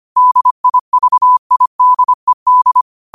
Morse code, heard as a keyed tone, about 25 wpm, 1000 Hz. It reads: DIVIDED